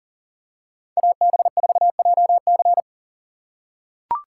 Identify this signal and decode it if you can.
Morse code, heard as a keyed tone, about 37 wpm, 700 Hz. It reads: AB4JC
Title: USA Call Signs - 5 Characters - 6x Character Spacing - 40wpm